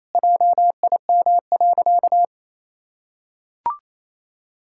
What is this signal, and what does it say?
Morse code, heard as a keyed tone, about 28 wpm, 700 Hz. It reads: JIM.
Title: Pangram Challenge v2 - 28wpm